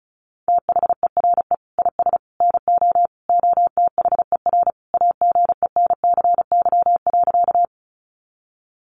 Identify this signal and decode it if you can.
Morse code, heard as a keyed tone, about 34 wpm, 700 Hz. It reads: THERE IS NO OTHER AGENCY.
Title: Sentences from Top 1300 Words 35wpm